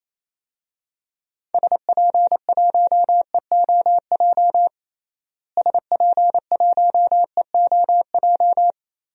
Morse code, heard as a keyed tone, about 28 wpm, 700 Hz. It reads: SP1EOJ SP1EOJ
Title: DXCC-100 Call Signs - 6 Characters - Repeated 3x - 28wpm